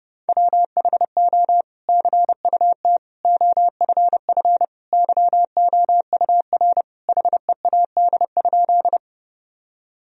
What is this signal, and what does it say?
Morse code, heard as a keyed tone, about 30 wpm, 700 Hz. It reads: WHO CUT OFF YOUR HEAD?
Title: Sentences from Top 1000 Words - Review Part 1 - 30wpm